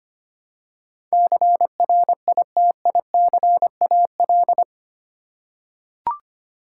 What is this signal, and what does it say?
Morse code, heard as a keyed tone, about 25 wpm, 700 Hz. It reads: CRITICAL
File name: Eight Letter Words - 25wpm